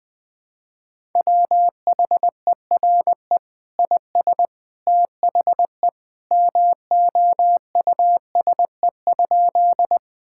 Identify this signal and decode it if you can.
Morse code, heard as a keyed tone, about 20 wpm, 700 Hz. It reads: WHERE IS THE MOUSE?
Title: Sentences from Top 1000 Words - Review Part 5 - 20wpm